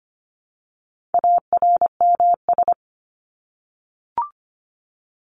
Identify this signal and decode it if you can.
Morse code, heard as a keyed tone, about 25 wpm, 700 Hz. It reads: ARMS
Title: ES:S Suffix - Top 100 Words with Base Words 25wpm